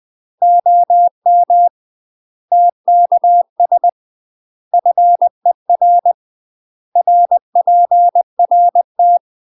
Morse code, heard as a keyed tone, about 20 wpm, 700 Hz. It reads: OM TKS FER RPRT
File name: UK MASTER QSOs TRAINING / 20WPM / UK MASTER QSOs 0002